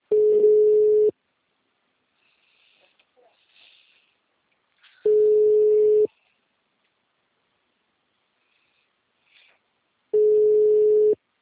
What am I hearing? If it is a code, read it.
Ringback tone (Europe, 425 Hz).
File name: out- -2003- - - .160